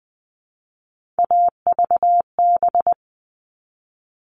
Morse code, heard as a keyed tone, about 20 wpm, 700 Hz. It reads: AVB
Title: All Cut Numbers - 3-Digit 20wpm